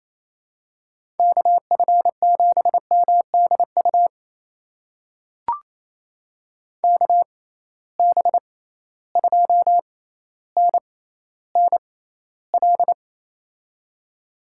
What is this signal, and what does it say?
Morse code, heard as a keyed tone, about 28 wpm, spaced as for 14 wpm, 700 Hz. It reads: KF7MDU KB2NNL
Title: USA Call Signs - 6 Characters - 6x Character Spacing - 28wpm